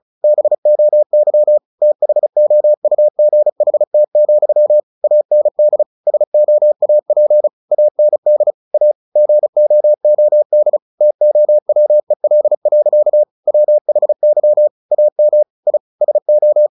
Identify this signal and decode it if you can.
Morse code, heard as a keyed tone, about 34 wpm, 600 Hz. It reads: BOY THOUGHT, AND SOAP AND A GOOD TOWEL. WHY AM I SO
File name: the old man and the sea / the old man and the sea 1 35wpm